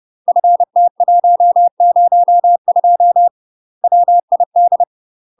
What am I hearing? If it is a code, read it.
Morse code, heard as a keyed tone, about 30 wpm, 700 Hz. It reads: FT102 WID
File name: UK MASTER QSOs TRAINING / 30WPM / UK MASTER QSOs 0005